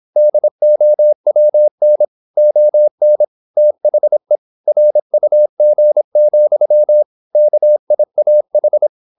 Morse code, heard as a keyed tone, about 26 wpm, 600 Hz. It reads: DOWN ON THE RUG, KIAH
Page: Never 26wpm | CW med Gnister